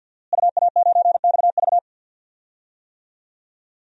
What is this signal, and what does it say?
Morse code, heard as a keyed tone, about 46 wpm, 700 Hz. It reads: UA9XV
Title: DXCC-100 Call Signs - 5 Characters - Repeated 3x - 50wpm